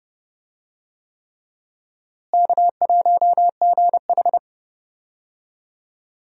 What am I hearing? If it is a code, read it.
Morse code, heard as a keyed tone, about 32 wpm, 700 Hz. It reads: K1GH
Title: USA Call Signs - 4 Characters - Repeated 3x - 30wpm